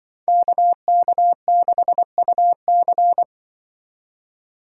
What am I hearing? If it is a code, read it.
Morse code, heard as a keyed tone, about 25 wpm, 700 Hz. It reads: KK6UC